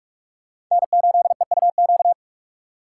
Morse code, heard as a keyed tone, about 46 wpm, 690 Hz. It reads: N8EUQ
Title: USA Call Signs - 5 Characters - 4x Character Spacing - 45wpm